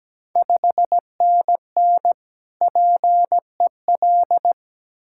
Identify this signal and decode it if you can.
Morse code, heard as a keyed tone, about 17 wpm, 700 Hz. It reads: 5NN PEL
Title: Ontario QSO Party Exchange 17wpm